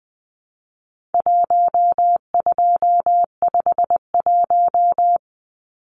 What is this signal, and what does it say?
Morse code, heard as a keyed tone, about 20 wpm, 700 Hz. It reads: 1251
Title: Numbers - Contest Style - 4-Digit 20wpm